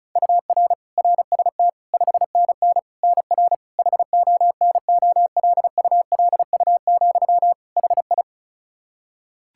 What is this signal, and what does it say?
Morse code, heard as a keyed tone, about 36 wpm, 700 Hz. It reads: UR RST 5NN NR HONOLULU, HI
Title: Practice QSO - Exchange 1 - Beginning - 35wpm